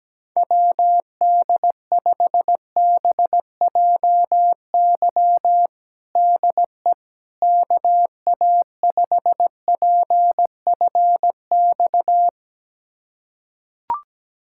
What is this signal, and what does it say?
Morse code, heard as a keyed tone, about 17 wpm, 700 Hz. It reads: WD5BJY DE KA5PFX
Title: USA Callsign DE Callsign - 6 Characters - 17wpm